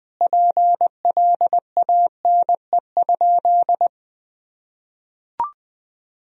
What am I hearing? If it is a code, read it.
Morse code, heard as a keyed tone, about 21 wpm, 700 Hz. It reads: PLANE?
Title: Sentences from Top 1000 Words - Review Part 3 - 20wpm